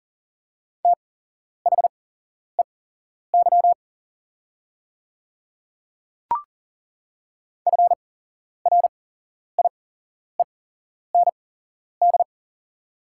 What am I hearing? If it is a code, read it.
Morse code, heard as a keyed tone, about 40 wpm, spaced as for 11 wpm, 700 Hz. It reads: THEY FRIEND
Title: 8x Character Spacing - Top 300 Words - 40wpm